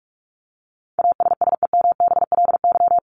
Morse code, heard as a keyed tone, about 44 wpm, 700 Hz. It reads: ASSEMBLY
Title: LY Suffix - Top 100 Words with Base Words 45wpm